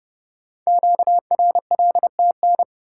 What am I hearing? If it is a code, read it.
Morse code, heard as a keyed tone, about 29 wpm, 700 Hz. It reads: QRLTN
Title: Code Groups - 5 Letters - 30wpm